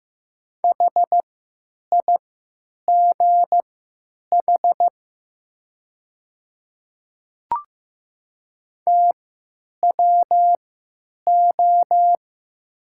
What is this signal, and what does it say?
Morse code, heard as a keyed tone, about 15 wpm, spaced as for 9 wpm, 700 Hz. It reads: HIGH TWO
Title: 3x Character Spacing - Top 300 Words - 15wpm